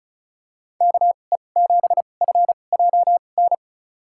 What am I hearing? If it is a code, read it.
Morse code, heard as a keyed tone, about 36 wpm, spaced as for 25 wpm, 700 Hz. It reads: KE7FJN
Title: USA Call Signs - 6 Characters - 2x Character Spacing - 35wpm